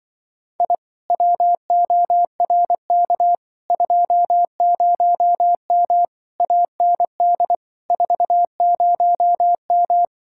Morse code, heard as a keyed tone, about 24 wpm, 700 Hz. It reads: I WORK 20M AND 40M
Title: Speed-Racing Common QSO Phrases Encore - 20wpm